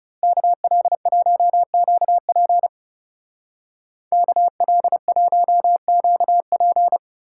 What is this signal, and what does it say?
Morse code, heard as a keyed tone, about 37 wpm, 700 Hz. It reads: KL1QP KL1QP
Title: Speed Racing - USA 2x2 Call Signs 30wpm